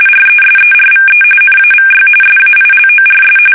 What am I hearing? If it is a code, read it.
RTTY, about 45.45 baud. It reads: RESERVE NAVY SAILORS
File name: fsk850 / battleshipnj